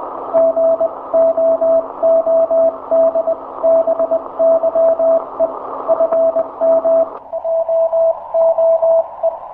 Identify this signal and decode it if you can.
Morse code, heard as a keyed tone, about 21 wpm, spaced as for 18 wpm, 660 Hz. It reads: GOODBYE FM OOE
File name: hamradio / last 500khz cw